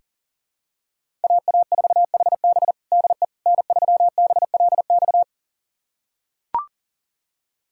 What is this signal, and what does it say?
Morse code, heard as a keyed tone, about 40 wpm, 700 Hz. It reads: AA4HB DE N3BLX
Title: USA Callsign DE Callsign - 5 Characters - 40wpm